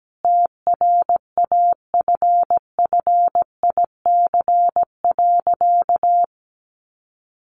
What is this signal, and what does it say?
Morse code, heard as a keyed tone, about 17 wpm, 700 Hz. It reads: TRAFFIC.
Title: Sentences from Top 1300 Words 17wpm